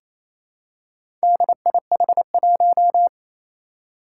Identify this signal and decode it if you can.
Morse code, heard as a keyed tone, about 29 wpm, 700 Hz. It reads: DIH1
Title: Four Challenging Letter-Number Combinations 28wpm